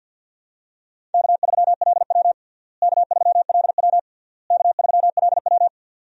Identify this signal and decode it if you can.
Morse code, heard as a keyed tone, about 51 wpm, 700 Hz. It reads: K3LW K3LW K3LW
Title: USA Call Signs - 4 Characters - Repeated 3x - 50wpm